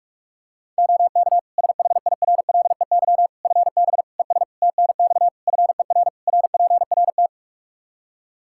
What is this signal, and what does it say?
Morse code, heard as a keyed tone, about 46 wpm, 700 Hz. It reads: OK SHIRLEY FB ES TNX FER RPRT
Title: Practice QSO - Exchange 2 - Info - 45wpm